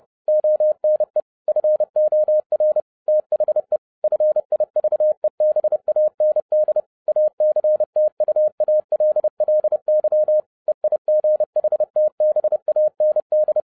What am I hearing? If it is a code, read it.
Morse code, heard as a keyed tone, about 30 wpm, 610 Hz. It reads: ONE FOR THE FIVEBAND ACTUALLY EIGHTBAND
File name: 2012-01-mp3 / n12-I DIDNT MISS THE SOLAR CYCLE 24-30wpm